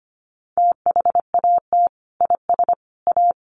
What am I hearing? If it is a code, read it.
Morse code, heard as a keyed tone, about 25 wpm, 700 Hz. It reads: THAT IS A
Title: Sentences from Top 1200 Words 25wpm